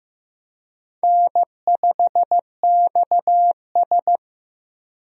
Morse code, heard as a keyed tone, about 15 wpm, 700 Hz. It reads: N5XS
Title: USA Call Signs - 4 Characters - Repeated 2x - 15wpm